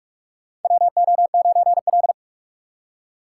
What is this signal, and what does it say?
Morse code, heard as a keyed tone, about 46 wpm, 700 Hz. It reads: WO9L
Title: USA Call Signs - 4 Characters - 4x Character Spacing - 45wpm